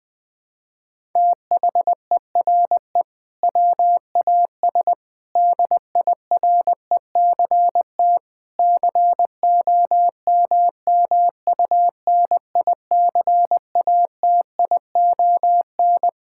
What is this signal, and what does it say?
Morse code, heard as a keyed tone, about 20 wpm, 700 Hz. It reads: THERE WAS DIRECT COMMUNICATION
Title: Sentences from Top 1000 Words - Review Part 6 - 20wpm